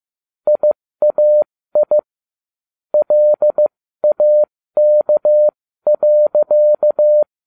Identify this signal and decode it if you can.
Morse code, heard as a keyed tone, about 15 wpm, spaced as for 13 wpm, 600 Hz. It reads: IAI LAK.
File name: cw / ubacw12 1